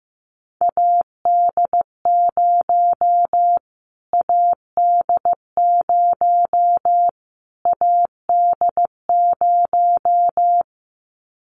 Morse code, heard as a keyed tone, about 15 wpm, 700 Hz. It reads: AD0 AD0 AD0
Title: Callsign Prefixes for Top 25 DXCC - Repeated 3x - 15wpm